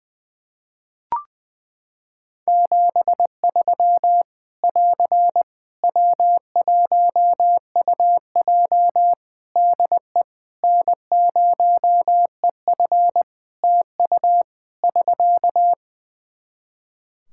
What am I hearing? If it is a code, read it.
Morse code, heard as a keyed tone, about 20 wpm, 700 Hz. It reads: 73 <AR> W1UJ DE N0EF TU <SK>
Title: Practice QSO - Exchange 4 - Ending - 20wpm